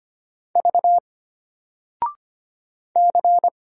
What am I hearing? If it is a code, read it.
Morse code, heard as a keyed tone, about 25 wpm, 700 Hz. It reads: V C
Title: Single Character (Letter:Number:Punctuation:Prosign) Alt 25wpm